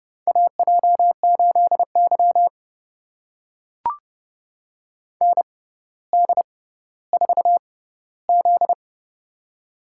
Morse code, heard as a keyed tone, about 30 wpm, spaced as for 15 wpm, 700 Hz. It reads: AJ8Y ND4Z
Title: USA Call Signs - 4 Characters - 6x Character Spacing - 30wpm